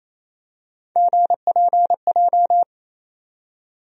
Morse code, heard as a keyed tone, about 28 wpm, 700 Hz. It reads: GPJ
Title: USA Call Sign Postfixes - 3 Letters 28wpm